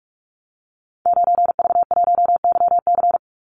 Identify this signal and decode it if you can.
Morse code, heard as a keyed tone, about 44 wpm, 700 Hz. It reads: 9V1YC
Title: Call Signs - 45wpm